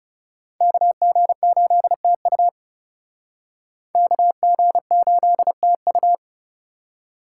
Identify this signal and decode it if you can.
Morse code, heard as a keyed tone, about 37 wpm, 700 Hz. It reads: KG8TU KG8TU
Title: Speed Racing - USA 2x2 Call Signs 30wpm